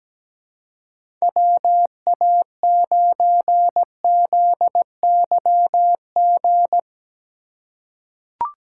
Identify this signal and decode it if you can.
Morse code, heard as a keyed tone, about 17 wpm, 700 Hz. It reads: WA9ZYG
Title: USA Call Signs - 2x Character Spacing - 17wpm